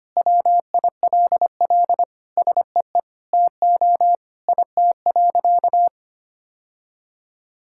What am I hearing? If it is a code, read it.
Morse code, heard as a keyed tone, about 25 wpm, 700 Hz. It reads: WILL SEE TO IT.
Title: Sentences from Top 1000 Words - Review Part 5 - 25wpm